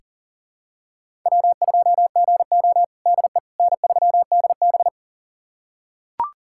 Morse code, heard as a keyed tone, about 40 wpm, 700 Hz. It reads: W2GO DE N3DB
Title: USA Callsign DE Callsign - 4 Characters - 40wpm